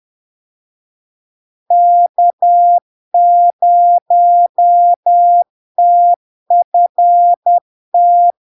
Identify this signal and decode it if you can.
Morse code, heard as a keyed tone, about 10 wpm, 700 Hz. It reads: K0TFT
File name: LARK-FD-2022-Log-in-CW-10wpm-700-Hz